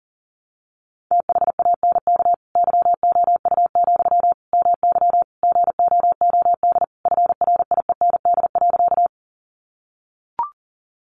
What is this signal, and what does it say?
Morse code, heard as a keyed tone, about 37 wpm, 700 Hz. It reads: THANK YOU, MY GOOD FRIEND.
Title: Sentences from Top 600 Words 40wpm